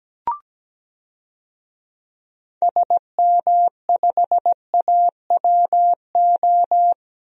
Morse code, heard as a keyed tone, about 17 wpm, 700 Hz. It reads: SM5AWO
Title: DXCC-100 Call Signs - 6 Characters - 17wpm